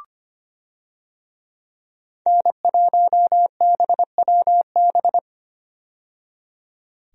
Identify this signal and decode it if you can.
Morse code, heard as a keyed tone, about 25 wpm, 700 Hz. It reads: N1BWB
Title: USA Call Signs - 5 Characters - 25wpm